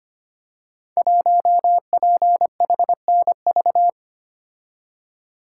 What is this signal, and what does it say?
Morse code, heard as a keyed tone, about 25 wpm, 700 Hz. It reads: 1PHNV
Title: Code Groups - 5 Letter:Numbers - 25wpm